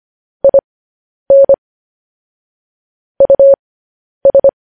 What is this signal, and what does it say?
Morse code, heard as a keyed tone, about 25 wpm, spaced as for 10 wpm, 550 Hz. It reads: IN US